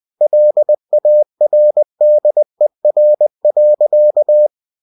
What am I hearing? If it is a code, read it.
Morse code, heard as a keyed tone, about 20 wpm, 600 Hz. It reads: LARDER.
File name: cw-books / winnie-the-pooh / winnie-20wpm-page-0015